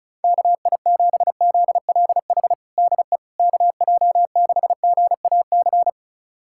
Morse code, heard as a keyed tone, about 34 wpm, 700 Hz. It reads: KI7ZLH DE KJ6GAC
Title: USA Callsign DE Callsign - 6 Characters - 35wpm